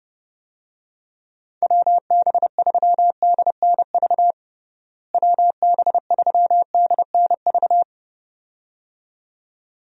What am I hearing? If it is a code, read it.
Morse code, heard as a keyed tone, about 30 wpm, 700 Hz. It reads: WB3DNV WB3DNV
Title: USA Call Signs - 6 Characters - Repeated 2x - 30wpm